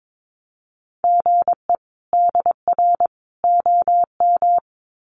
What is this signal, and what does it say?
Morse code, heard as a keyed tone, about 22 wpm, 700 Hz. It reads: GE DR OM
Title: Common QSO Phrases 22wpm